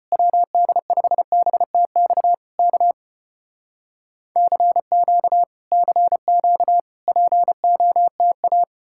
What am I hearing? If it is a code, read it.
Morse code, heard as a keyed tone, about 35 wpm, 700 Hz. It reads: WD5BTX K CQ CQ POTA